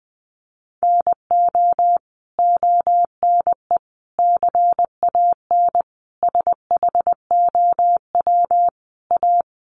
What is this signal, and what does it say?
Morse code, heard as a keyed tone, about 20 wpm, 700 Hz. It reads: NO ONE CAN SHOW A
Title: Sentences from Top 1500 Words 20wpm